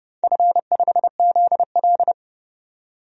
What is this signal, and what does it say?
Morse code, heard as a keyed tone, about 32 wpm, 700 Hz. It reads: F5ZL
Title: DXCC-100 Call Signs - 4 Characters - 30wpm